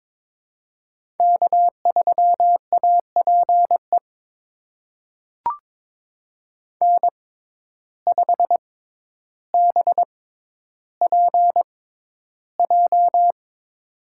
Morse code, heard as a keyed tone, about 22 wpm, spaced as for 11 wpm, 700 Hz. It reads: K3APE N5BPJ
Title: USA Call Signs - 5 Characters - 6x Character Spacing - 22wpm